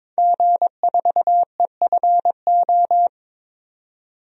Morse code, heard as a keyed tone, about 21 wpm, 700 Hz. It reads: G4EFO
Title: DXCC-100 Call Signs - 5 Characters - 22wpm